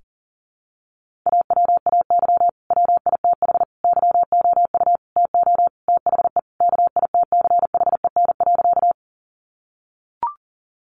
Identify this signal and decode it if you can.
Morse code, heard as a keyed tone, about 40 wpm, 700 Hz. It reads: AWAY WITH YOU TO THE KITCHEN.
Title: Sentences from Top 1400 Words 40wpm